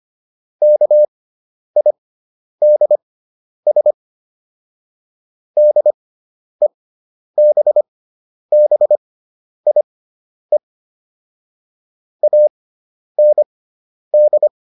Morse code, heard as a keyed tone, about 25 wpm, spaced as for 10 wpm, 600 Hz. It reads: KIDS DEBBIE AND